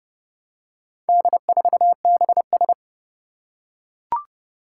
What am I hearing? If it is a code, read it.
Morse code, heard as a keyed tone, about 30 wpm, 700 Hz. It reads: D4BS
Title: Four Challenging Letter-Number Combinations 30wpm